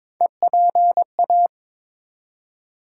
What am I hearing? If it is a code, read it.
Morse code, heard as a keyed tone, about 22 wpm, 700 Hz. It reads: EPA
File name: ARRL Sections v2 - 22wpm